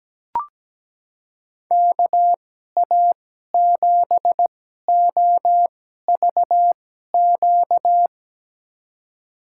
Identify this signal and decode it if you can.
Morse code, heard as a keyed tone, about 17 wpm, spaced as for 12 wpm, 700 Hz. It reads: KA7OVQ
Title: USA Call Signs - 6 Characters - 2x Character Spacing - 17wpm